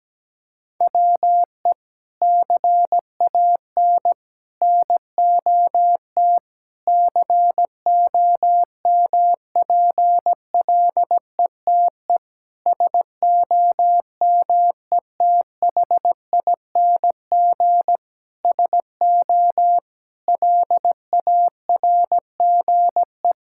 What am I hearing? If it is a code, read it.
Morse code, heard as a keyed tone, about 17 wpm, 700 Hz. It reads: WE CAN NOT COMPLETE SOMETHING SO LARGE
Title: Sentences from Top 1000 Words - Review Part 6 - 17wpm